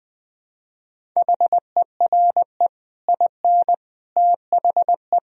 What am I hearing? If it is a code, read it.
Morse code, heard as a keyed tone, about 20 wpm, 700 Hz. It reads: HERE IN THE
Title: Speed-Racing - Sets of 5 Words 20wpm